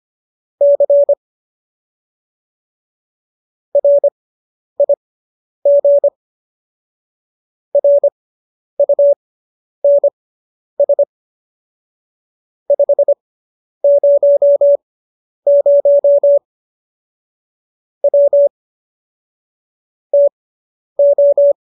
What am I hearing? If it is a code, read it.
Morse code, heard as a keyed tone, about 25 wpm, spaced as for 10 wpm, 570 Hz. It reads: C RIG RUNS 500 W TO